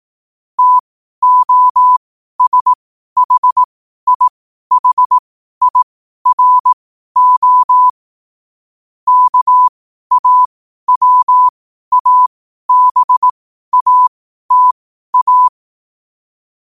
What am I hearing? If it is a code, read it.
Morse code, heard as a keyed tone, about 18 wpm, spaced as for 12 wpm, 1000 Hz. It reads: TOSHIHIRO KAWABATA
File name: qotdfiles / 12wpm